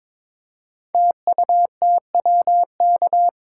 Morse code, heard as a keyed tone, about 23 wpm, 690 Hz. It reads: TUTWK
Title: Code Groups - 5 Letters - 22wpm